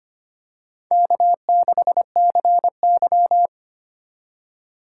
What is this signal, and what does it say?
Morse code, heard as a keyed tone, about 25 wpm, 690 Hz. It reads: K6CY
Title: USA Call Signs - 4 Characters - 4x Character Spacing - 25wpm